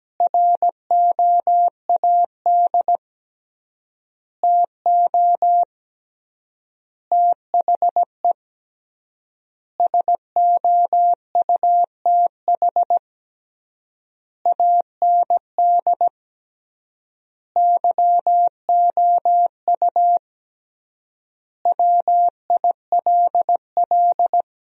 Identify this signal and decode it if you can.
Morse code, heard as a keyed tone, about 17 wpm, 700 Hz. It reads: ROAD TO THE SOUTH AND YOU WILL
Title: 3x Word-Spacing - Sentences from Top 400 Words - 17wpm